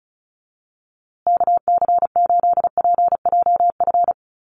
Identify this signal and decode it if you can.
Morse code, heard as a keyed tone, about 34 wpm, 700 Hz. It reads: KC8PJF
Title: USA Call Signs Repeated 3x 35wpm